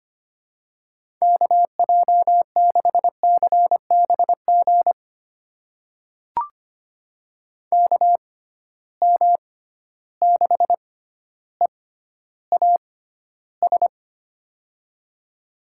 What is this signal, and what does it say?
Morse code, heard as a keyed tone, about 25 wpm, spaced as for 13 wpm, 700 Hz. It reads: KJ6CBG KM6EAS
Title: USA Call Signs - 6 Characters - 6x Character Spacing - 25wpm